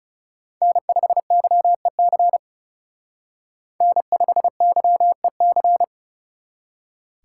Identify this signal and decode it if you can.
Morse code, heard as a keyed tone, about 37 wpm, 700 Hz. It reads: N5YEC N5YEC
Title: Speed Racing - USA 1x3 Call Signs 30wpm